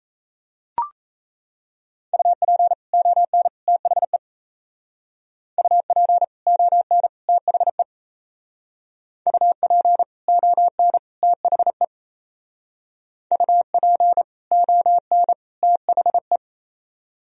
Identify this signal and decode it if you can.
Morse code, heard as a keyed tone, about 36 wpm, 700 Hz. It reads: UP ON THE UP ON THE UP ON THE UP ON THE